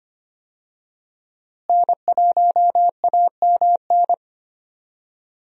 Morse code, heard as a keyed tone, about 25 wpm, 700 Hz. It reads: N1AMN
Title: USA Call Signs - 5 Characters - Repeated 3x - 25wpm